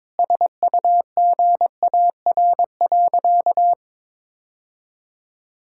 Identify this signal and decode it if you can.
Morse code, heard as a keyed tone, about 22 wpm, 700 Hz. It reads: SUGAR.
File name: practice / 22wpm / Sentences with 4-Word Phrases - 22wpm